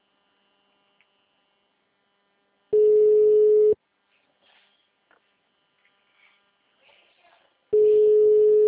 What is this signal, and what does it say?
Ringback tone (Europe, 425 Hz).